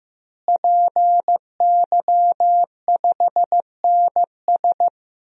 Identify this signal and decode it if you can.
Morse code, heard as a keyed tone, about 15 wpm, 700 Hz. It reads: PY5NS
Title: DXCC-100 Call Signs - 5 Characters - Repeated 3x - 15wpm